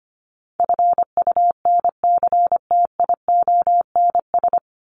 Morse code, heard as a keyed tone, about 25 wpm, 700 Hz. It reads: FUNCTIONS
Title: ES:S Suffix - Top 100 Words with Base Words 25wpm